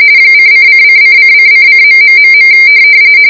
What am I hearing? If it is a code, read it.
RTTY, about 45.45 baud. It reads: :::::::::....